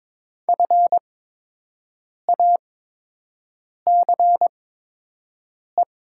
Morse code, heard as a keyed tone, about 22 wpm, spaced as for 6 wpm, 700 Hz. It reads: FACE
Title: 8x Character Spacing - Top 500 Words - 22wpm